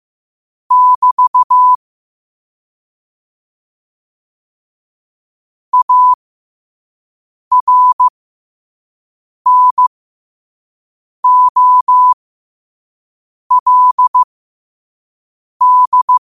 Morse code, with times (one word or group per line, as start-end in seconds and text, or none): 0.7-1.8 <BT>
5.7-16.2 ARNOLD